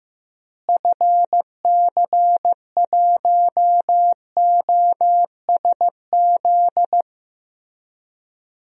Morse code, heard as a keyed tone, about 15 wpm, 700 Hz. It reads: FC1OSZ
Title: DXCC-100 Call Signs - 6 Characters - Repeated 3x - 15wpm